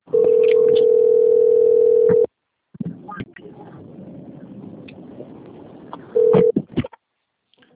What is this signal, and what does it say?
Ringback tone (North America), 440 + 480 Hz.